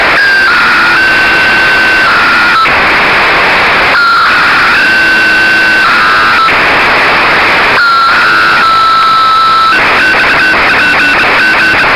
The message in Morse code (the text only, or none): none